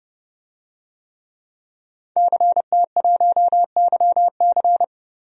Morse code, heard as a keyed tone, about 30 wpm, 700 Hz. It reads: CT1YC
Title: DXCC-100 Call Signs - 5 Characters - 30wpm